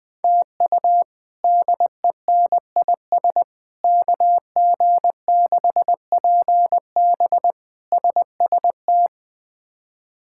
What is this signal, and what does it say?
Morse code, heard as a keyed tone, about 20 wpm, 700 Hz. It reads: TU DENIS KG6PB SST
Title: K1USN SST QSO Practice - 20wpm